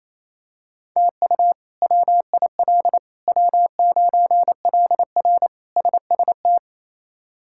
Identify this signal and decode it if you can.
Morse code, heard as a keyed tone, about 28 wpm, 700 Hz. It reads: TU WIL W9LR SST
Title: K1USN SST QSO Practice - 28wpm